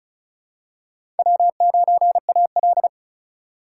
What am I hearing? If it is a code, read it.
Morse code, heard as a keyed tone, about 36 wpm, 700 Hz. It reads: W9AL
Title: USA Call Signs - 4x Character Spacing - 35wpm